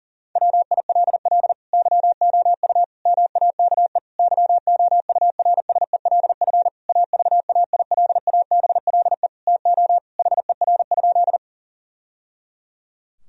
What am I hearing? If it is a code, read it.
Morse code, heard as a keyed tone, about 37 wpm, 700 Hz. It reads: WILL YOU MAKE YOURSELF AVAILABLE TO HER?
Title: Sentences from Top 1000 Words - Review Part 3 - 40wpm